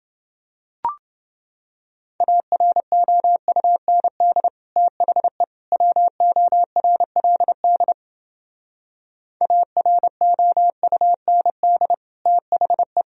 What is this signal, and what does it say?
Morse code, heard as a keyed tone, about 30 wpm, 700 Hz. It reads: AROUND THE WORLD AROUND THE